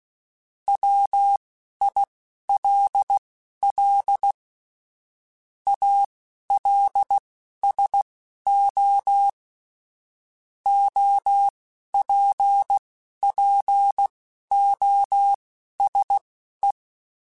Morse code, heard as a keyed tone, about 16 wpm, spaced as for 11 wpm, 800 Hz. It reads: WILL ALSO OPPOSE
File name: Technical Subjects / Technical Subjects-mp3s / XL @10